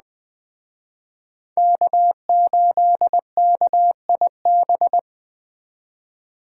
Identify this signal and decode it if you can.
Morse code, heard as a keyed tone, about 20 wpm, 700 Hz. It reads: K8KIB
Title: Speed Racing - USA 1x3 Call Signs 17wpm